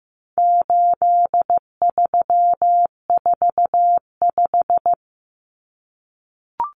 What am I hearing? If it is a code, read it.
Morse code, heard as a keyed tone, about 15 wpm, 700 Hz. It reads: 8345
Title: Common Cut Numbers - 4-Digit 15wpm